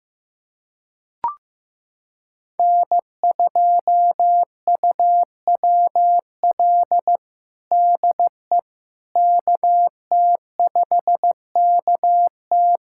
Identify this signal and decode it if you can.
Morse code, heard as a keyed tone, about 15 wpm, 700 Hz. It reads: N2UWL DE KT5KT
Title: USA Callsign DE Callsign - 5 Characters - 15wpm